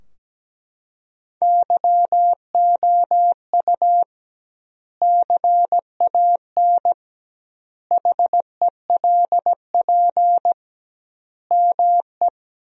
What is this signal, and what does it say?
Morse code, heard as a keyed tone, about 17 wpm, 700 Hz. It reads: YOU CAN HELP ME
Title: 2x Word-Spacing - Sentences from Top 200 Words - 17wpm